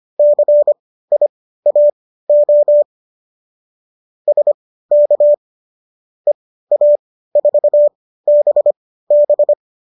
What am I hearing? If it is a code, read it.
Morse code, heard as a keyed tone, about 25 wpm, spaced as for 15 wpm, 590 Hz. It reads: CIAO SK EA4BB